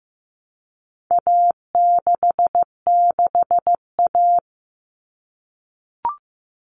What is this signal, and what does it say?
Morse code, heard as a keyed tone, about 15 wpm, 700 Hz. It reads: A66A
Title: All Cut Numbers - 4-Digit 15wpm